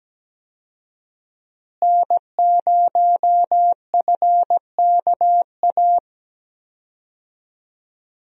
Morse code, heard as a keyed tone, about 17 wpm, 700 Hz. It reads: N0FKA
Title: USA Call Signs - 5 Characters - Repeated 3x - 17wpm